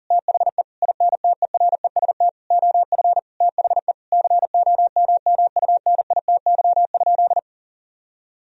Morse code, heard as a keyed tone, about 40 wpm, 700 Hz. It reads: THE INTEREST OF THE COMMUNITY?
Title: Sentences from Top 1000 Words - Review Part 4 - 40wpm